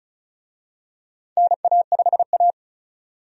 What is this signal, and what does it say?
Morse code, heard as a keyed tone, about 36 wpm, 700 Hz. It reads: NA5A
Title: USA Call Signs - 4 Characters - 35wpm